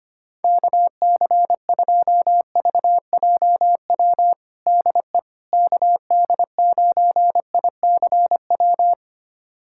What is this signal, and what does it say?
Morse code, heard as a keyed tone, about 25 wpm, 700 Hz. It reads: KC2VJW DE KD9ICW
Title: USA Callsign DE Callsign 25wpm